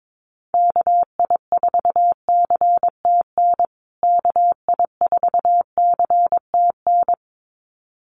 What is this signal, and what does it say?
Morse code, heard as a keyed tone, about 22 wpm, 700 Hz. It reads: KI4CTN KI4CTN
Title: USA Call Signs Repeated 2x 22wpm